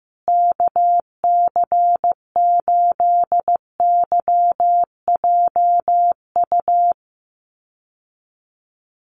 Morse code, heard as a keyed tone, about 14 wpm, 700 Hz. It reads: KC8YJU
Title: USA Call Signs Repeated 3x 15wpm